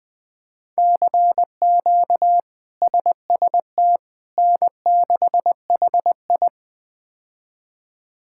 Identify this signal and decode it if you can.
Morse code, heard as a keyed tone, about 20 wpm, 700 Hz. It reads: CQ SST N6HI
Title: K1USN SST QSO Practice - 20wpm